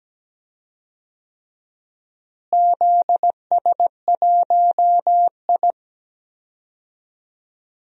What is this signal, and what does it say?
Morse code, heard as a keyed tone, about 17 wpm, 700 Hz. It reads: ZS1I
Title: DXCC-100 Call Signs - 4 Characters - Repeated 3x - 17wpm